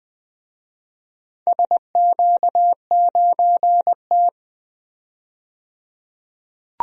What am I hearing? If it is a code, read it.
Morse code, heard as a keyed tone, about 20 wpm, 700 Hz. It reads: SQ9T
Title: DXCC-100 Call Signs - 4 Characters - 20wpm